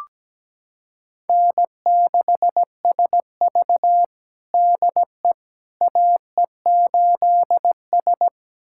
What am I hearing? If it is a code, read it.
Morse code, heard as a keyed tone, about 17 wpm, 700 Hz. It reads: N6SV DE AE8S
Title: USA Callsign DE Callsign - 4 Characters - 17wpm